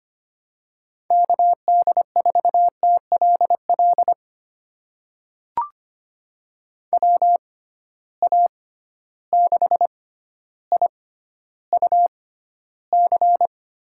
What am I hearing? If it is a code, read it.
Morse code, heard as a keyed tone, about 25 wpm, spaced as for 13 wpm, 700 Hz. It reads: KD4TLL WA6IUC
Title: USA Call Signs - 6 Characters - 6x Character Spacing - 25wpm